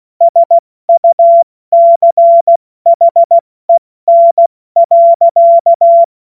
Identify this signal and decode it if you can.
Morse code, heard as a keyed tone, about 16 wpm, spaced as for 14 wpm, 680 Hz. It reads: SUCHEN.
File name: Morsekurs / Witze / Witz 20. Januar 2026 70 F80